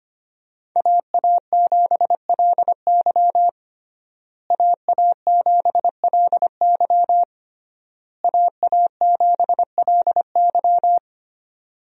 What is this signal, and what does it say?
Morse code, heard as a keyed tone, about 25 wpm, 700 Hz. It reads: AA7LY AA7LY AA7LY
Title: USA Call Signs - 5 Characters - Repeated 3x - 25wpm